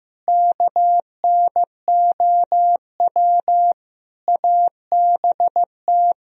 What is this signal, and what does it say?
Morse code, heard as a keyed tone, about 15 wpm, 700 Hz. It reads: KNOW ABT
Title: Sentences from Top 100 Words and Abbreviations 15wpm